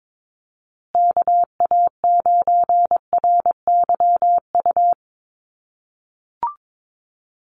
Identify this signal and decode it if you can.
Morse code, heard as a keyed tone, about 22 wpm, 700 Hz. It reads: KA9RYU
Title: USA Call Signs Repeated 2x 22wpm